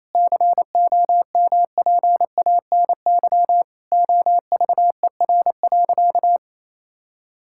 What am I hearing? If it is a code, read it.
Morse code, heard as a keyed tone, about 28 wpm, 700 Hz. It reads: COMPANY OVER.
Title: Sentences from Top 1000 Words - Review Part 6 - 28wpm